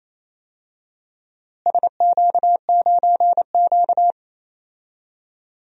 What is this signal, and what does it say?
Morse code, heard as a keyed tone, about 28 wpm, 700 Hz. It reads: SQ9Q
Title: DXCC-100 Call Signs - 4 Characters - Repeated 2x - 28wpm